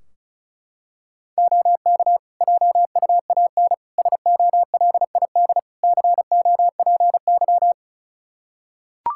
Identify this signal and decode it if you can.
Morse code, heard as a keyed tone, about 36 wpm, 700 Hz. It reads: OK JUAN SOLID COPY
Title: K1USN SST QSO Practice 35wpm